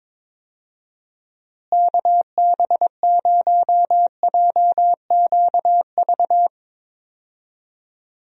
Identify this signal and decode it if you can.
Morse code, heard as a keyed tone, about 22 wpm, 700 Hz. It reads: KB0JQV
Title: USA Call Signs - 6 Characters - Repeated 3x - 22wpm